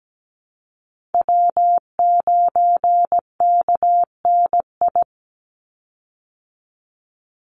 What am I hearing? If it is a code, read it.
Morse code, heard as a keyed tone, about 17 wpm, 700 Hz. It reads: W9KNI
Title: Call Signs Repeated 3x - 17wpm